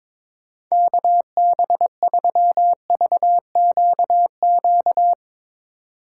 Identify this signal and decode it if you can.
Morse code, heard as a keyed tone, about 22 wpm, 700 Hz. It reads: KB3VQQ
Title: USA Call Signs - 6 Characters - 2x Character Spacing - 22wpm